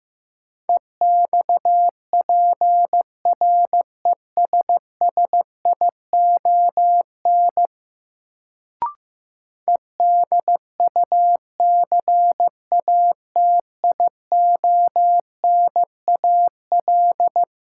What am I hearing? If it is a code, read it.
Morse code, heard as a keyed tone, about 15 wpm, 700 Hz. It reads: EXPRESSION EDUCATIONAL
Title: Sentences from Top 1200 Words - New Word Drill Alt - 15wpm